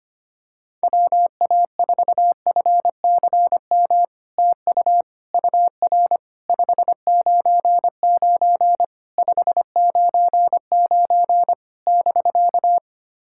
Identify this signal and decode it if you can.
Morse code, heard as a keyed tone, about 25 wpm, 700 Hz. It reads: WA4FCM TU UR 599 599 <BK>
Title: SOTA QSO Practice 25wpm